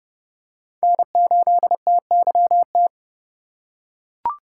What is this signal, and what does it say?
Morse code, heard as a keyed tone, about 30 wpm, 700 Hz. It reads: N8TYT
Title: USA Call Signs - 5 Characters - 3x Character Spacing - 30wpm